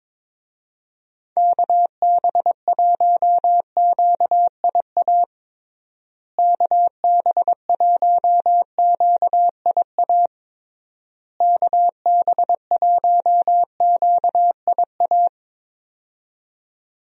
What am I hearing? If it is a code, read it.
Morse code, heard as a keyed tone, about 22 wpm, 700 Hz. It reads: KB1QIA KB1QIA KB1QIA
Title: USA Call Signs - 6 Characters - Repeated 3x - 22wpm